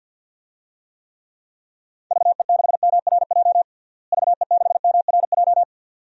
Morse code, heard as a keyed tone, about 47 wpm, 700 Hz. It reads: VE6MRJ VE6MRJ
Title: DXCC-100 Call Signs - 6 Characters - Repeated 2x - 50wpm